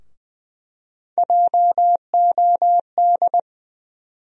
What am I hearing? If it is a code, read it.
Morse code, heard as a keyed tone, about 20 wpm, 700 Hz. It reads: JOD
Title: USA Call Sign Postfixes - 3 Letters 20wpm